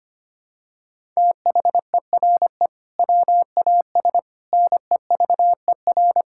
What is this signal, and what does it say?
Morse code, heard as a keyed tone, about 25 wpm, 700 Hz. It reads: THERE WAS NEVER
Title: Sentences from Top 1000 Words - Review Part 6 - 25wpm